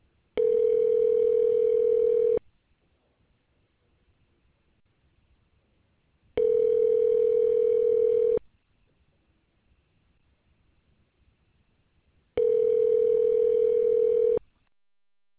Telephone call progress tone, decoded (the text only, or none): ringback tone